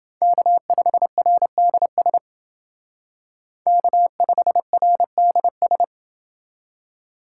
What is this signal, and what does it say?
Morse code, heard as a keyed tone, about 29 wpm, 700 Hz. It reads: K5RDS K5RDS